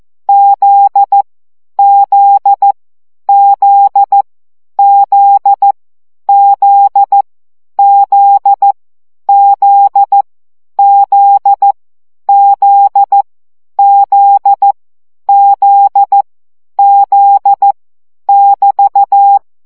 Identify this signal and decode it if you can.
Morse code, heard as a keyed tone, about 14 wpm, 800 Hz. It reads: Z Z Z Z Z Z Z Z Z Z Z Z <BT>
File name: CW gyakorlas z 12 baud